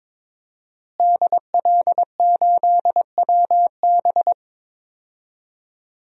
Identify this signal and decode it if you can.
Morse code, heard as a keyed tone, about 23 wpm, 700 Hz. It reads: DL8WB
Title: DXCC-100 Call Signs - 5 Characters - Repeated 2x - 22wpm